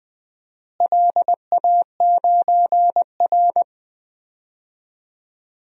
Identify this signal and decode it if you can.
Morse code, heard as a keyed tone, about 20 wpm, 700 Hz. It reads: LA9R
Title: DXCC-100 Call Signs - 4 Characters - Repeated 2x - 20wpm